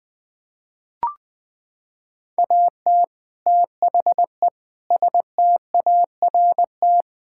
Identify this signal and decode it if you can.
Morse code, heard as a keyed tone, about 20 wpm, 700 Hz. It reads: AT THE START
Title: Sets of 6 Words - 20wpm